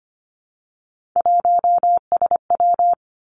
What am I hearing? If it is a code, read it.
Morse code, heard as a keyed tone, about 25 wpm, 700 Hz. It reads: 1SW
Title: Three Challenging Letter-Number Combinations 25wpm